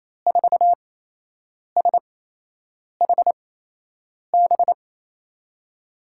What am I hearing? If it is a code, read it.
Morse code, heard as a keyed tone, about 28 wpm, spaced as for 8 wpm, 700 Hz. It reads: 4SHB
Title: USA Call Signs - 8x Character Spacing - 28wpm